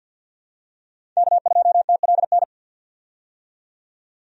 Morse code, heard as a keyed tone, about 52 wpm, 700 Hz. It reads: K2TLN
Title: USA Call Signs - 5 Characters - 50wpm